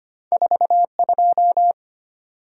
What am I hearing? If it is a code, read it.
Morse code, heard as a keyed tone, about 34 wpm, spaced as for 30 wpm, 690 Hz. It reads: 42
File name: practice / 25wpm / Speed-Racing - Two Letter-Number Combinations - 25wpm